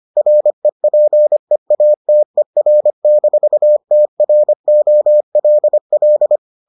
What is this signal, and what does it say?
Morse code, heard as a keyed tone, about 25 wpm, 600 Hz. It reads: REPEATER-TROLL